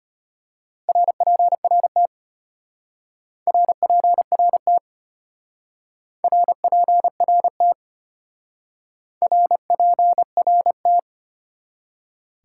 Morse code, heard as a keyed tone, about 31 wpm, 700 Hz. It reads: RPRT RPRT RPRT RPRT